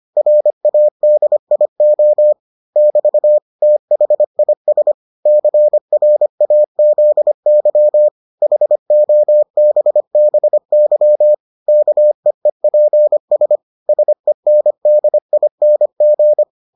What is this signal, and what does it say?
Morse code, heard as a keyed tone, about 25 wpm, 600 Hz. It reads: RADIO <BT> THIS CRAZY HOBBY KEEPS SENDING